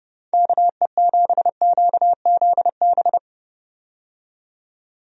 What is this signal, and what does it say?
Morse code, heard as a keyed tone, about 30 wpm, 700 Hz. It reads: KE7QZB
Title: USA Call Signs - 6 Characters - 30wpm